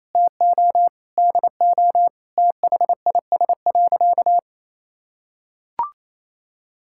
Morse code, heard as a keyed tone, about 28 wpm, 700 Hz. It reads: TO DO THIS.
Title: Sentences from Top 1000 Words - Review Part 5 - 28wpm